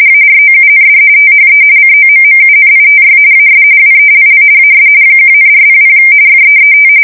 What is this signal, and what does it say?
RTTY, about 45.45 baud. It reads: AROUND THE COMMUNITY AS WELL AS 1,400 FEET
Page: 170 Shift 2125/2295